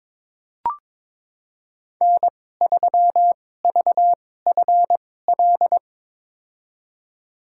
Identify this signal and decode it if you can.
Morse code, heard as a keyed tone, about 22 wpm, spaced as for 16 wpm, 700 Hz. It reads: N3VFL
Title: USA Call Signs - 2x Character Spacing - 22wpm